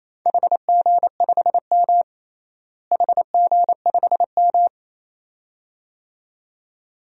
Morse code, heard as a keyed tone, about 28 wpm, 700 Hz. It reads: HG5M HG5M
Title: DXCC-100 Call Signs - 4 Characters - Repeated 2x - 28wpm